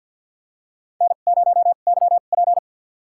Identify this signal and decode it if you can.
Morse code, heard as a keyed tone, about 50 wpm, spaced as for 36 wpm, 700 Hz. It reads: N0YP
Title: USA Call Signs - 2x Character Spacing - 50wpm